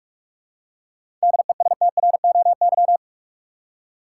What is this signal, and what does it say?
Morse code, heard as a keyed tone, about 42 wpm, 700 Hz. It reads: DESTROY
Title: Seven Letter Words - 45wpm